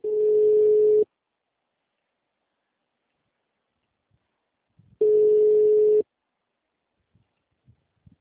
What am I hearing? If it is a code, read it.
Ringback tone (Europe, 425 Hz).